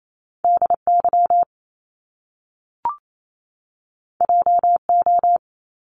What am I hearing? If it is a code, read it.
Morse code, heard as a keyed tone, about 28 wpm, 700 Hz. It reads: DY JO
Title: Top Two Letter Combinations in Words 28wpm